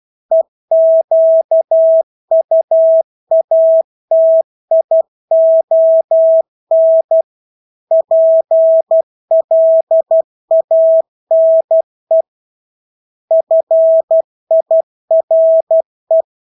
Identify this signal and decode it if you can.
Morse code, heard as a keyed tone, about 12 wpm, 650 Hz. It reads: EQUATION PLANE FIRE
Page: Engelske ord 12 wpm